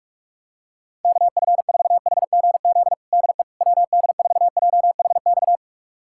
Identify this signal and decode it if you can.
Morse code, heard as a keyed tone, about 46 wpm, 700 Hz. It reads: KF4HGZ DE WD4JHX
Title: USA Callsign DE Callsign - 6 Characters - 45wpm